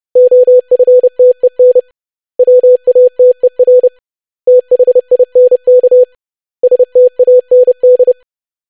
Morse code, heard as a keyed tone, about 30 wpm, 500 Hz. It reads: OFTEN WATER THINK STAND
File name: lesson / web / basic500-25wpm